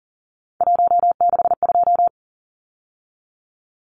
Morse code, heard as a keyed tone, about 40 wpm, 700 Hz. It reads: 162
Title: Numbers - 3-Digit 40wpm